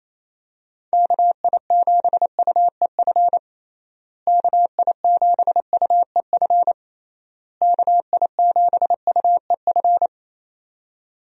Morse code, heard as a keyed tone, about 27 wpm, 700 Hz. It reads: KI7UEF KI7UEF KI7UEF
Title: USA Call Signs - 6 Characters - Repeated 3x - 28wpm